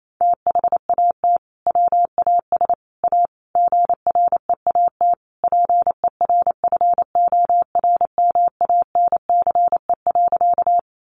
Morse code, heard as a keyed tone, about 28 wpm, 700 Hz. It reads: THAT WAS A GREAT PERFORMANCE.
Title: Sentences from Top 1000 Words 28wpm